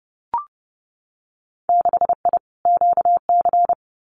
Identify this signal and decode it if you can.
Morse code, heard as a keyed tone, about 30 wpm, 700 Hz. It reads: 6I QC
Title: Winter Field Day Exchanges Encore 30wpm